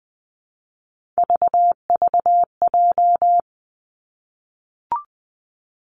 Morse code, heard as a keyed tone, about 20 wpm, 700 Hz. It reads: VVJ
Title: Three Challenging Letter-Number Combinations 20wpm